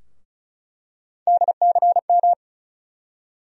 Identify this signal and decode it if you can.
Morse code, heard as a keyed tone, about 34 wpm, 700 Hz. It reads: DCM
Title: USA Call Sign Postfixes - 3 Letters 35wpm